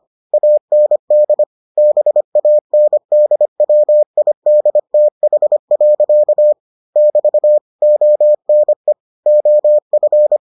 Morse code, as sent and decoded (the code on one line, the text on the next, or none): .- -. -.. / -... .- -. -.. .-- .. -.. - .... .-.-.- / -...- / --- -. / --- ..-.
AND BANDWIDTH. <BT> ONE OF